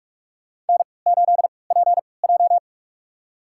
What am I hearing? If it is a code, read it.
Morse code, heard as a keyed tone, about 46 wpm, spaced as for 26 wpm, 700 Hz. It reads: N8PJ
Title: USA Call Signs - 4 Characters - 3x Character Spacing - 45wpm